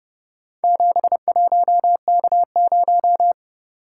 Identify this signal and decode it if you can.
Morse code, heard as a keyed tone, about 30 wpm, 700 Hz. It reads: 71K0